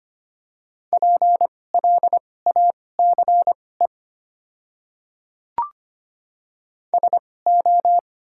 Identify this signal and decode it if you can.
Morse code, heard as a keyed tone, about 25 wpm, spaced as for 18 wpm, 700 Hz. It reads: PLACE SO
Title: 2x Character Spacing - Top 300 Words - 25wpm